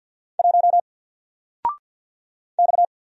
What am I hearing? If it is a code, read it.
Morse code, heard as a keyed tone, about 50 wpm, 700 Hz. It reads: 1 X
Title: Single Letter-Number 50wpm